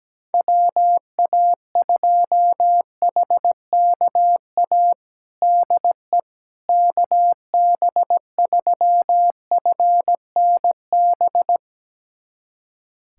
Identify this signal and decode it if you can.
Morse code, heard as a keyed tone, about 17 wpm, 700 Hz. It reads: WA2HKA DE KB3FNB
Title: USA Callsign DE Callsign - 6 Characters - 17wpm